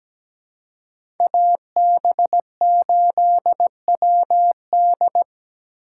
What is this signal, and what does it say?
Morse code, heard as a keyed tone, about 17 wpm, 700 Hz. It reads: AB8WD
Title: USA Call Signs - 5 Characters - 4x Character Spacing - 17wpm